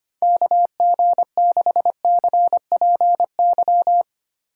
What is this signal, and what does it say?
Morse code, heard as a keyed tone, about 25 wpm, 700 Hz. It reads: KG6CPY
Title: USA Call Signs - 6 Characters - 2x Character Spacing - 25wpm